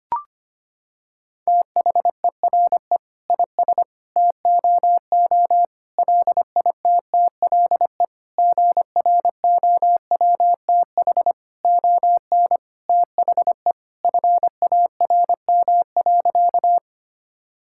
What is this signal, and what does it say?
Morse code, heard as a keyed tone, about 25 wpm, 700 Hz. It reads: THERE IS TOO LITTLE GROWTH ON THE FARM.
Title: Sentences from Top 1000 Words - Review Part 5 - 25wpm